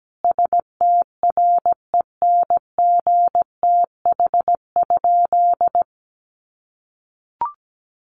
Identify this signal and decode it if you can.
Morse code, heard as a keyed tone, about 17 wpm, 700 Hz. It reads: STRENGTH?
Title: Sentences from Top 1300 Words 17wpm